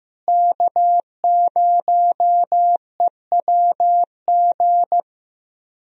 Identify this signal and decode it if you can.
Morse code, heard as a keyed tone, about 15 wpm, 700 Hz. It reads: K0EWG
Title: USA Call Signs - 5 Characters - Repeated 3x - 15wpm